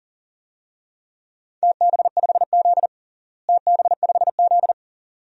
Morse code, heard as a keyed tone, about 40 wpm, 700 Hz. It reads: TB5Z TB5Z
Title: DXCC-100 Call Signs - 4 Characters - Repeated 2x - 40wpm